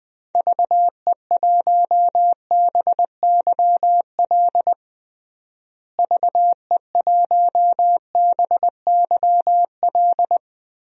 Morse code, heard as a keyed tone, about 20 wpm, 700 Hz. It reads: VE1BYL VE1BYL
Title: DXCC-100 Call Signs - 6 Characters - Repeated 3x - 20wpm